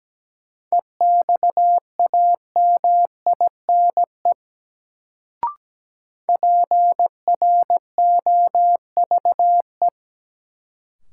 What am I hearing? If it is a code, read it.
Morse code, heard as a keyed tone, about 17 wpm, 700 Hz. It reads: EXAMINE PROVE
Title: Sentences from Top 1600 Words - New Word Drill Alt - 17wpm